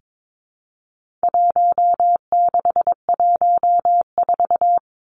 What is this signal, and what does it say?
Morse code, heard as a keyed tone, about 22 wpm, 700 Hz. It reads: 1614
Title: Numbers - Contest Style - 4-Digit 22wpm